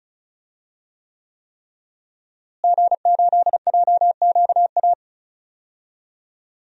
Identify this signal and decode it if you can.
Morse code, heard as a keyed tone, about 36 wpm, 700 Hz. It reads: G8JQA
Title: DXCC-100 Call Signs - 5 Characters - Repeated 3x - 35wpm